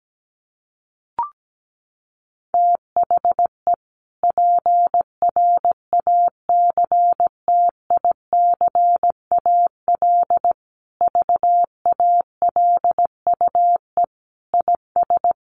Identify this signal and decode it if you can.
Morse code, heard as a keyed tone, about 17 wpm, 700 Hz. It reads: THE PRACTICAL VALUE IS
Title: Sentences from Top 1400 Words 17wpm